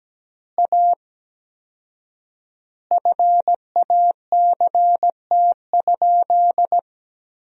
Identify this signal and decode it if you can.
Morse code, heard as a keyed tone, about 17 wpm, 700 Hz. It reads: A FACT?
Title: 4x Word-Spacing - Sentences from Top 500 Words - 17wpm